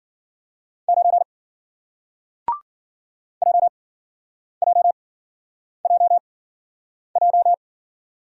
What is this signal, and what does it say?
Morse code, heard as a keyed tone, about 54 wpm, 700 Hz. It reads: 9 J J J J